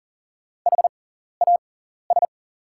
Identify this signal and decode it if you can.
Morse code, heard as a keyed tone, about 39 wpm, spaced as for 14 wpm, 700 Hz. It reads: HAS
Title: 6x Character Spacing - Top 100 Words - 40wpm